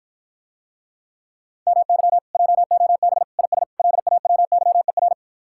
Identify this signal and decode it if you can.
Morse code, heard as a keyed tone, about 53 wpm, 700 Hz. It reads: MY JOB IS LAWYER